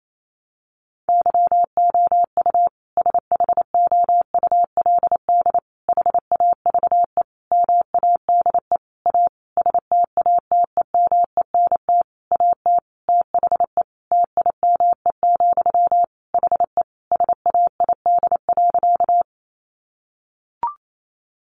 Morse code, heard as a keyed tone, about 28 wpm, 700 Hz. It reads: YOU SHOULD HAVE MADE A STATEMENT AT THE TIME, HE SAID.
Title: Sentences from Top 1200 Words 28wpm